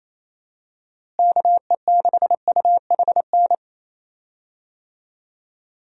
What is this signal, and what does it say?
Morse code, heard as a keyed tone, about 28 wpm, 700 Hz. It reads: KE6UHN
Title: USA Call Signs - 6 Characters - Repeated 3x - 28wpm